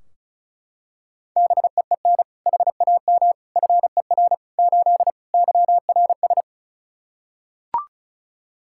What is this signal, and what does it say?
Morse code, heard as a keyed tone, about 36 wpm, 700 Hz. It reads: BEEN HAM FER 8 YRS
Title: Practice QSO - Exchange 3 - More Info - 35wpm